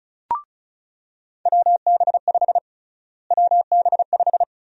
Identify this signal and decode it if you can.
Morse code, heard as a keyed tone, about 34 wpm, 700 Hz. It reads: WB5 WB5
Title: USA Call Sign Prefixes - Repeated 2x - 35wpm